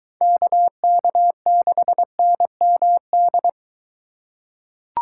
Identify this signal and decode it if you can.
Morse code, heard as a keyed tone, about 21 wpm, 700 Hz. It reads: KK6NMD